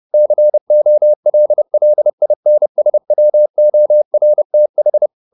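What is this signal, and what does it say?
Morse code, heard as a keyed tone, about 30 wpm, 600 Hz. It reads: COLLINSWORTH
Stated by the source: Morse code at 29 wpm, sending Ken Follett's novel Never in English, from CW med Gnister.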